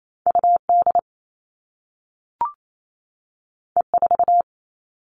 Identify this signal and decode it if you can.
Morse code, heard as a keyed tone, about 28 wpm, 700 Hz. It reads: UD E4
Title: All Cut Numbers - 2-Digit 28wpm